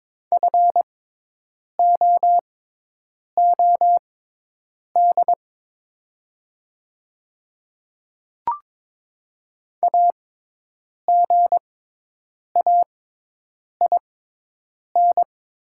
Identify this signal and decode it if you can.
Morse code, heard as a keyed tone, about 22 wpm, spaced as for 8 wpm, 700 Hz. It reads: FOOD AGAIN
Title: 6x Character Spacing - Top 300 Words - 22wpm